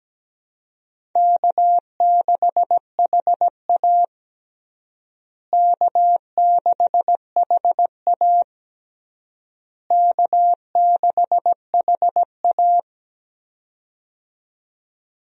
Morse code, heard as a keyed tone, about 17 wpm, 700 Hz. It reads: K6HA K6HA K6HA
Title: USA Call Signs - 4 Characters - Repeated 3x - 17wpm